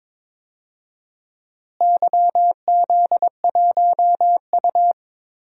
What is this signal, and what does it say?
Morse code, heard as a keyed tone, about 22 wpm, 700 Hz. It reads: YZ1U
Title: DXCC-100 Call Signs - 4 Characters - Repeated 2x - 22wpm